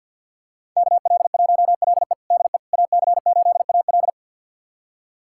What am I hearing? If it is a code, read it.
Morse code, heard as a keyed tone, about 51 wpm, 700 Hz. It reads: KL1LE DE AC8AL
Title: USA Callsign DE Callsign - 5 Characters - 50wpm